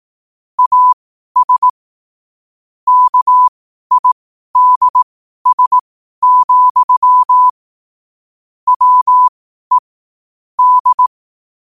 Morse code, heard as a keyed tone, about 18 wpm, spaced as for 12 wpm, 1000 Hz. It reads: AS KIDS, WE D
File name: qotdfiles / 12wpm